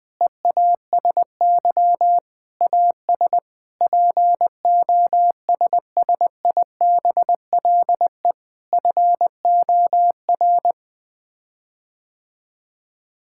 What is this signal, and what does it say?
Morse code, heard as a keyed tone, about 21 wpm, 700 Hz. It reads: EASY AS POSSIBLE FOR
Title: Sets of 8 Words - 20wpm